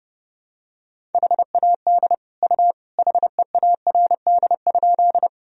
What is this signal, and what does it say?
Morse code, heard as a keyed tone, about 30 wpm, 700 Hz. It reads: HAD U HEARD?
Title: Sentences from Top 400 Words and Abbreviations 30wpm